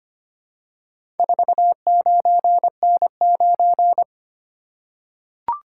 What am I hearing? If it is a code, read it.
Morse code, heard as a keyed tone, about 25 wpm, 700 Hz. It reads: 49N9
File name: practice / 25wpm / Four Letter-Number Combinations - 25wpm